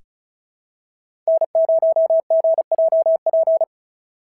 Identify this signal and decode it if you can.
Morse code, heard as a keyed tone, about 36 wpm, 650 Hz. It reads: N0GJP
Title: SOTA QSO Practice 35wpm